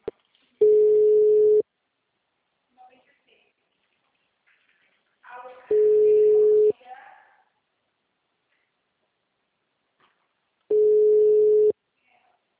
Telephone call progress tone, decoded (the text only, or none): ringback tone